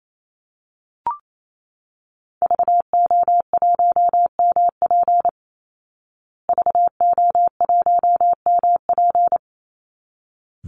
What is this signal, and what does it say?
Morse code, heard as a keyed tone, about 28 wpm, 700 Hz. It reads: VO1MP VO1MP
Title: Call Signs Repeated 3x - 28wpm